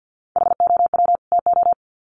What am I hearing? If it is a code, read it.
Morse code, heard as a keyed tone, about 50 wpm, 700 Hz. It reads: HOW TO
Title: Sets of 2 Words 50wpm